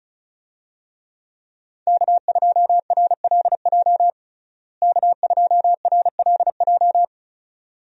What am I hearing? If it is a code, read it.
Morse code, heard as a keyed tone, about 36 wpm, 700 Hz. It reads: K2RLJ K2RLJ
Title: USA Call Signs - 5 Characters - Repeated 2x - 35wpm